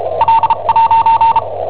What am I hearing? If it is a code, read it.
Morse code, heard as a keyed tone, about 34 wpm, spaced as for 29 wpm, 910 Hz. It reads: L'
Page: Cassetta esercizi preparata da I7OHP per IW5ELP